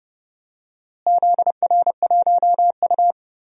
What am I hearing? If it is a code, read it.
Morse code, heard as a keyed tone, about 32 wpm, 700 Hz. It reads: ZR1U
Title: DXCC-100 Call Signs - 4 Characters - Repeated 2x - 30wpm